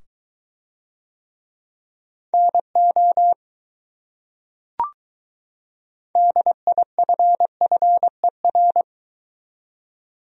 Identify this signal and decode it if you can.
Morse code, heard as a keyed tone, about 22 wpm, 700 Hz. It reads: NO DIFFER